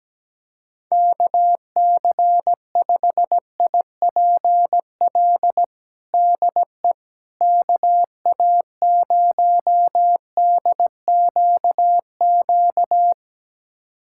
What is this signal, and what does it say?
Morse code, heard as a keyed tone, about 17 wpm, 700 Hz. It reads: KC5IPL DE KA0DQQ
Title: USA Callsign DE Callsign - 6 Characters - 17wpm